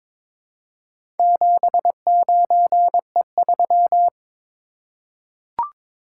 Morse code, heard as a keyed tone, about 22 wpm, 700 Hz. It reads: 79E3
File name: practice / 22wpm / Four Letter-Number Combinations - 22wpm